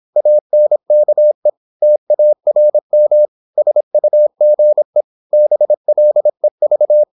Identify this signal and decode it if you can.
Morse code, heard as a keyed tone, about 26 wpm, 600 Hz. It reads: ANKE TARM SUGE BLEV
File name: Korte ord DK 26wpm